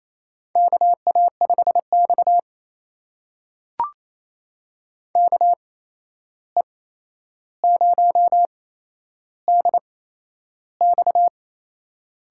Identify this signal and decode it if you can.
Morse code, heard as a keyed tone, about 28 wpm, spaced as for 11 wpm, 700 Hz. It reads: KA5X KE0DX
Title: USA Call Signs - 8x Character Spacing - 28wpm